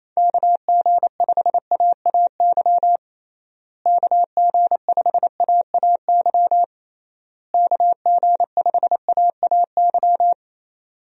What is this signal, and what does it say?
Morse code, heard as a keyed tone, about 28 wpm, 700 Hz. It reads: KG5AAY KG5AAY KG5AAY
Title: USA Call Signs - 6 Characters - Repeated 3x - 28wpm